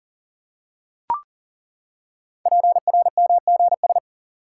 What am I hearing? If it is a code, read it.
Morse code, heard as a keyed tone, about 39 wpm, 700 Hz. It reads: PRMGS
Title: Code Groups - 5 Letters - 40wpm